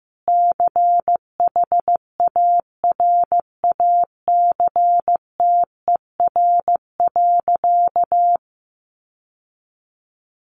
Morse code, heard as a keyed tone, about 15 wpm, 700 Hz. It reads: CHARACTER.
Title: Sentences from Top 1200 Words 15wpm